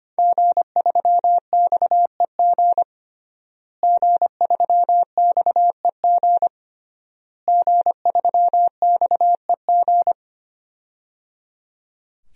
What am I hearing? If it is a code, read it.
Morse code, heard as a keyed tone, about 25 wpm, 700 Hz. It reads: G3XEG G3XEG G3XEG
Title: DXCC-100 Call Signs - 5 Characters - Repeated 3x - 25wpm